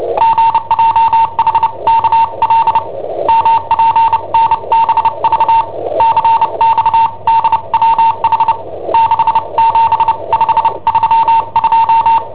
Morse code, heard as a keyed tone, about 29 wpm, spaced as for 25 wpm, 910 Hz. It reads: GJHKL MPNBV CXDWH 67532
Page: Cassetta esercizi preparata da I7OHP per IW5ELP